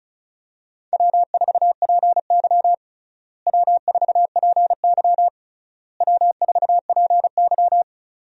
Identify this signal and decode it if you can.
Morse code, heard as a keyed tone, about 36 wpm, 700 Hz. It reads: W4PY W4PY W4PY
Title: USA Call Signs - 4 Characters - Repeated 3x - 35wpm